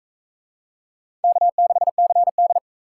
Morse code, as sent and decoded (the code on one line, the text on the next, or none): -.- -.... -.-. -..
K6CD